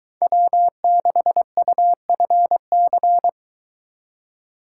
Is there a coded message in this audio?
Yes: Morse code from 0.2 to 3.3 s, heard as a keyed tone, about 24 wpm, 700 Hz. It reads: W6UFC